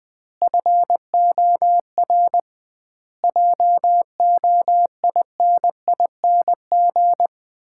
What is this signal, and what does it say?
Morse code, heard as a keyed tone, about 21 wpm, 700 Hz. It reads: FOR JOINING
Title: 2x Word-Spacing - Sets of 5 Words - 20wpm